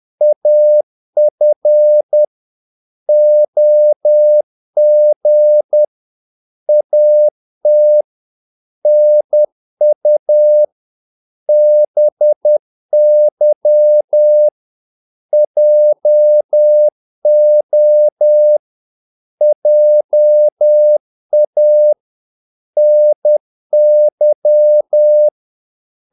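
Morse code, heard as a keyed tone, about 10 wpm, 600 Hz. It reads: AF OG AT NU BY JO JA NY
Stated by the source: Morse practice of short Danish words at 10 wpm, from CW med Gnister.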